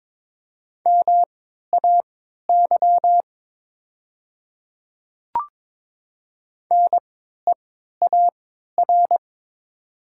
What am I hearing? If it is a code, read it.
Morse code, heard as a keyed tone, about 22 wpm, spaced as for 12 wpm, 700 Hz. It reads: MAY NEAR
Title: 3x Character Spacing - Top 200 Words - 22wpm